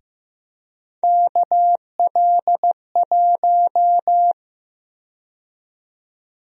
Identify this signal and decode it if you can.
Morse code, heard as a keyed tone, about 15 wpm, 700 Hz. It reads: KL1
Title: USA Call Sign Prefixes - Repeated 2x - 15wpm